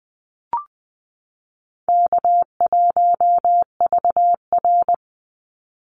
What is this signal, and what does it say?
Morse code, heard as a keyed tone, about 20 wpm, 700 Hz. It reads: K1VR
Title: Easy Call Signs - 20wpm